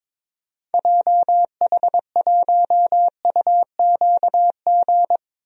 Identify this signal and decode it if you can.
Morse code, heard as a keyed tone, about 22 wpm, 700 Hz. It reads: JH1UQG
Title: DXCC-100 Call Signs - 6 Characters - Repeated 3x - 22wpm